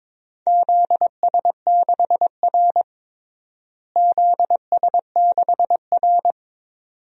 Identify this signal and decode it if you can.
Morse code, heard as a keyed tone, about 21 wpm, 700 Hz. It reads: ZS6R ZS6R
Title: DXCC-100 Call Signs - 4 Characters - Repeated 2x - 22wpm